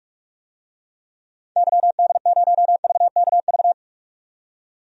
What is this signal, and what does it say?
Morse code, heard as a keyed tone, about 47 wpm, 700 Hz. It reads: YD0VKV
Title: DXCC-100 Call Signs - 6 Characters - 45wpm